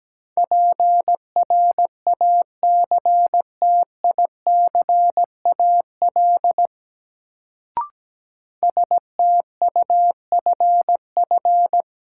Morse code, heard as a keyed tone, about 17 wpm, 700 Hz. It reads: PRACTICAL STUFF
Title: Sentences from Top 1400 Words - New Word Drill - 17wpm